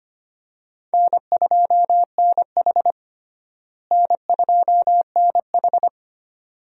Morse code, heard as a keyed tone, about 25 wpm, 700 Hz. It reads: N2NH N2NH
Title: USA Call Signs - 4 Characters - Repeated 2x - 25wpm